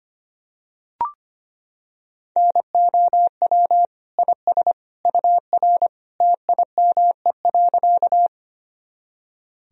Morse code, heard as a keyed tone, about 25 wpm, 700 Hz. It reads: NOW IS UR TIME.
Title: Sentences from Top 100 Words and Abbreviations 25wpm